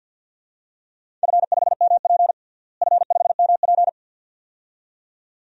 Morse code, heard as a keyed tone, about 47 wpm, 700 Hz. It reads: F5MP F5MP
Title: DXCC-100 Call Signs - 4 Characters - Repeated 2x - 50wpm